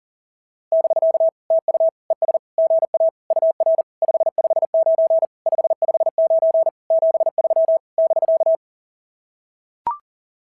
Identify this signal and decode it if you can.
Morse code, heard as a keyed tone, about 37 wpm, 660 Hz. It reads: <BK> TU ES GA UR 559 559 73 <BK>
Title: SOTA QSO Practice 40wpm